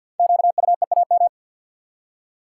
Morse code, heard as a keyed tone, about 50 wpm, 700 Hz. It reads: QUEAM
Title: Code Groups - 5 Letters - 50wpm